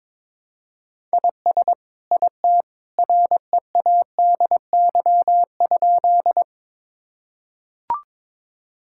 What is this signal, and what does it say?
Morse code, heard as a keyed tone, about 22 wpm, 700 Hz. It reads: IS IT READY?
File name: practice / 22wpm / Sentences with 2-Word Phrases - 22wpm